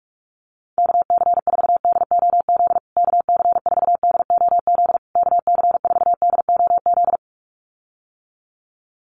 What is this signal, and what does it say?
Morse code, heard as a keyed tone, about 44 wpm, 700 Hz. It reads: KC4DOZ KC4DOZ KC4DOZ
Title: USA Call Signs Repeated 3x 45wpm